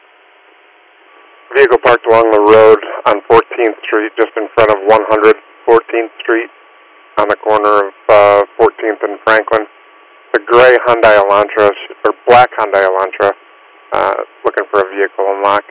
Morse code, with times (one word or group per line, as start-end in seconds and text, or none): none